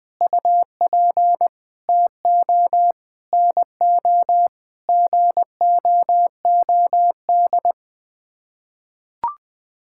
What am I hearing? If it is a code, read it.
Morse code, heard as a keyed tone, about 21 wpm, 700 Hz. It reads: UP TO NO GOOD
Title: Sets of 9 Words - 20wpm